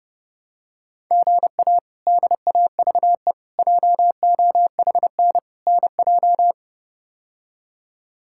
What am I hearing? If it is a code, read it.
Morse code, heard as a keyed tone, about 30 wpm, 700 Hz. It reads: GA DAVE JOHN NJ
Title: K1USN SST QSO Practice - 30wpm